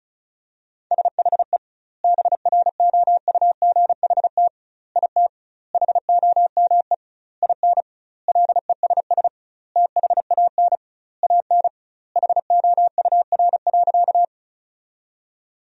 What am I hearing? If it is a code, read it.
Morse code, heard as a keyed tone, about 36 wpm, 700 Hz. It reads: SHE BROUGHT IT HOME IN LESS THAN AN HOUR.
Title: 2x Word-Spacing - Sentences from Top 500 Words - 35wpm